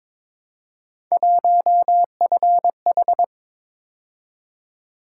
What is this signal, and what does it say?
Morse code, heard as a keyed tone, about 26 wpm, 700 Hz. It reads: 1FH
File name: Speed-Racing - Three Letter-Number Combinations - 22wpm